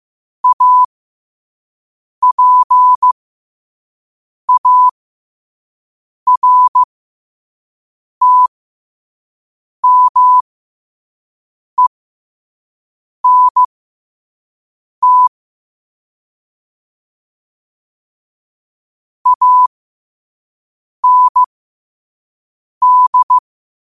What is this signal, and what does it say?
Morse code, heard as a keyed tone, about 15 wpm, spaced as for 5 wpm, 1000 Hz. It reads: APARTMENT AND
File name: qotdfiles / 5wpm